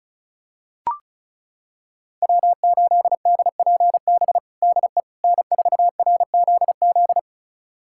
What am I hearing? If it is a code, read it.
Morse code, heard as a keyed tone, about 34 wpm, 690 Hz. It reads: W8DPB DE N4RZZ
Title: USA Callsign DE Callsign - 5 Characters - 35wpm